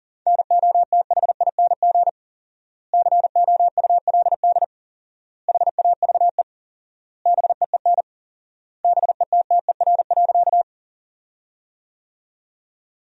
Morse code, heard as a keyed tone, about 40 wpm, 700 Hz. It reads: NOTHING COULD HAVE BEEN BETTER.
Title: 4x Word-Spacing - Sentences from Top 500 Words - 40wpm